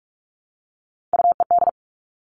Morse code, heard as a keyed tone, about 46 wpm, 700 Hz. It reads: UED
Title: Three Challenging Letter-Number Combinations 45wpm